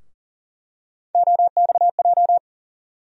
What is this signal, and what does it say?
Morse code, heard as a keyed tone, about 40 wpm, 700 Hz. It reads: OXJ
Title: USA Call Sign Postfixes - 3 Letters 40wpm